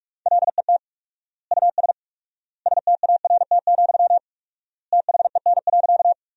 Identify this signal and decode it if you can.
Morse code, heard as a keyed tone, about 46 wpm, 700 Hz. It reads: LET US START, THEN.
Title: 4x Word-Spacing - Sentences from Top 300 Words - 45wpm